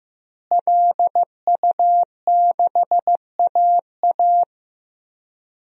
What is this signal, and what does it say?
Morse code, heard as a keyed tone, about 15 wpm, 700 Hz. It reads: LU6AA
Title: DXCC-100 Call Signs - 5 Characters - Repeated 3x - 15wpm